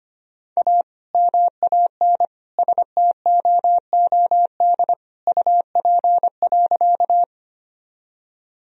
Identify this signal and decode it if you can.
Morse code, heard as a keyed tone, about 25 wpm, 700 Hz. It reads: A MAN STOOD UP.
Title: Sentences from Top 1000 Words - Review Part 2 - 25wpm